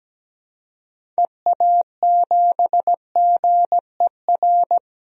Morse code, heard as a keyed tone, about 17 wpm, 700 Hz. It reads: EA7GER
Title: DXCC-100 Call Signs - 6 Characters - 17wpm